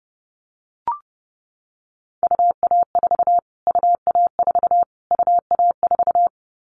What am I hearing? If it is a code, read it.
Morse code, heard as a keyed tone, about 30 wpm, 690 Hz. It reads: UA4 UA4 UA4
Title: Callsign Prefixes for Top 25 DXCC - Repeated 3x - 30wpm